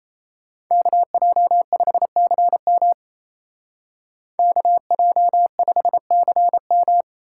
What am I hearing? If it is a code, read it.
Morse code, heard as a keyed tone, about 34 wpm, 700 Hz. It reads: KJ5CM KJ5CM
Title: Speed Racing - USA 2x2 Call Signs 28wpm